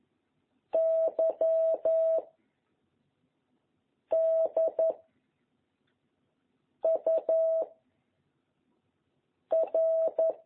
Morse code, heard as a keyed tone, about 11 wpm, 660 Hz. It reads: Y D U R
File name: jh8cbh / 2021cw